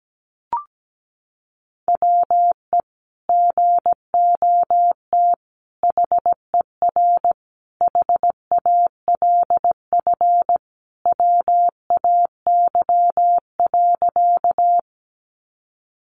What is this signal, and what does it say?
Morse code, heard as a keyed tone, about 17 wpm, 700 Hz. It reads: WE GOT HER HALF WAY.
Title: Sentences from Top 400 Words 17wpm